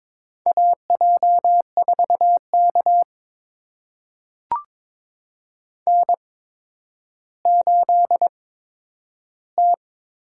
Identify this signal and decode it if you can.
Morse code, heard as a keyed tone, about 22 wpm, spaced as for 11 wpm, 700 Hz. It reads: AJ4K N8T
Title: USA Call Signs - 8x Character Spacing - 22wpm